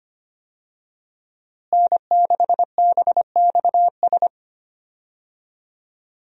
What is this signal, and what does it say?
Morse code, heard as a keyed tone, about 25 wpm, 700 Hz. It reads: N6BXS
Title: USA Call Signs - 5 Characters - Repeated 3x - 25wpm